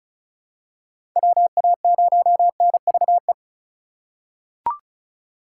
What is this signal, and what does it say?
Morse code, heard as a keyed tone, about 36 wpm, 700 Hz. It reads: WA0NVE
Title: USA Call Signs - 6 Characters - 3x Character Spacing - 35wpm